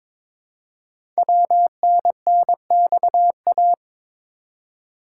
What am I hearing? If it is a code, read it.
Morse code, heard as a keyed tone, about 22 wpm, 700 Hz. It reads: WNNXA
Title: Code Groups - 5 Letters - 22wpm